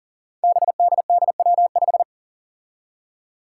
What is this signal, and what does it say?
Morse code, heard as a keyed tone, about 40 wpm, 700 Hz. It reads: BDDW5
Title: Five Challenging Letter-Number Combinations 40wpm